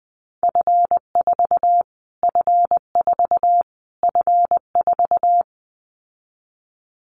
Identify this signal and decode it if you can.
Morse code, heard as a keyed tone, about 20 wpm, 700 Hz. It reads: F4 F4 F4
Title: Callsign Prefixes for Top 25 DXCC - Repeated 3x - 20wpm